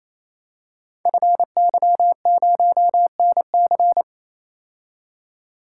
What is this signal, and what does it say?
Morse code, heard as a keyed tone, about 29 wpm, 700 Hz. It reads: FY0NC
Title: Code Groups - 5 Letter:Numbers - 28wpm